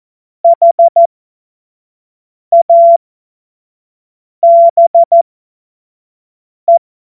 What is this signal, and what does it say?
Morse code, heard as a keyed tone, about 14 wpm, spaced as for 5 wpm, 680 Hz. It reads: HABE
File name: Morsekurs / Witze / Witz 08. Februar 2022 25 F70